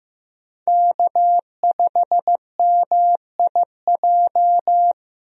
Morse code, heard as a keyed tone, about 15 wpm, 700 Hz. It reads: K5MIJ
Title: USA Call Signs - 5 Characters - Repeated 3x - 15wpm